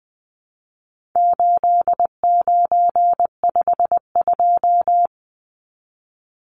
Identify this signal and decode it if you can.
Morse code, heard as a keyed tone, about 20 wpm, 700 Hz. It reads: 8952
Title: Numbers - 4-Digit 20wpm